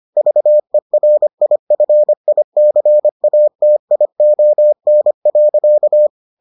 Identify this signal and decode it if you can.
Morse code, heard as a keyed tone, about 25 wpm, 600 Hz. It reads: VERIFICATION.